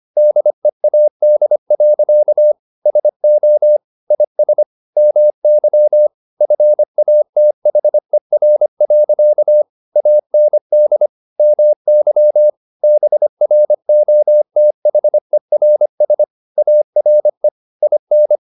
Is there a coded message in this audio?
Yes: Morse code from 0.2 to 18.4 s, heard as a keyed tone, about 25 wpm, 600 Hz. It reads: DEAD. SO IS MY FATHER. AND MY BROTHERS ARE IN